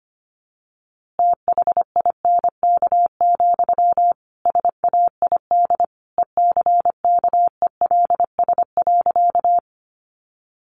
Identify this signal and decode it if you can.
Morse code, heard as a keyed tone, about 25 wpm, 700 Hz. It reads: THINK, SAID ECKELS.
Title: A Sound Of Thunder 25wpm